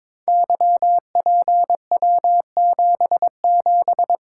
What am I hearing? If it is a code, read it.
Morse code, heard as a keyed tone, about 22 wpm, 700 Hz. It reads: YPW77
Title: Code Groups - 5 Letter:Numbers - 22wpm